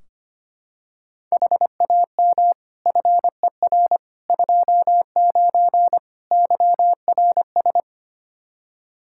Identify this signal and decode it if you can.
Morse code, heard as a keyed tone, about 25 wpm, 700 Hz. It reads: HAM FER 29 YRS
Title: Practice QSO - Exchange 3 - More Info - 25wpm